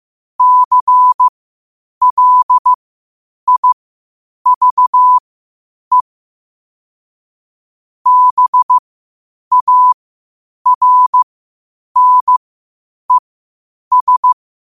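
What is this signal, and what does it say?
Morse code, heard as a keyed tone, about 15 wpm, spaced as for 8 wpm, 1000 Hz. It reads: CLIVE BARNES
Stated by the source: Morse code at 8 wpm.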